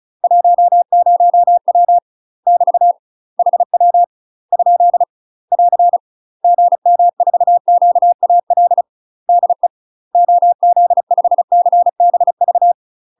Morse code, heard as a keyed tone, about 36 wpm, 700 Hz. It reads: 10W <BT> HW ? <AR> GM4QAL DE OZ5CBV
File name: UK MASTER QSOs TRAINING / 35WPM / UK MASTER QSOs 0011